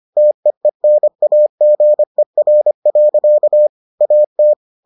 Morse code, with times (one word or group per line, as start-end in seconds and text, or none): 0.2-3.7 TEENAGER.
4.0-4.5 AT